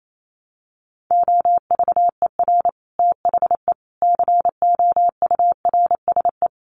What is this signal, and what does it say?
Morse code, heard as a keyed tone, about 28 wpm, 700 Hz. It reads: OVER THE COURSE
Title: Sets of 5 Words 28wpm